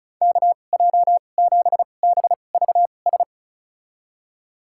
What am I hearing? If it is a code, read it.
Morse code, heard as a keyed tone, about 36 wpm, spaced as for 25 wpm, 700 Hz. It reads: KJ7BVS
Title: USA Call Signs - 2x Character Spacing - 35wpm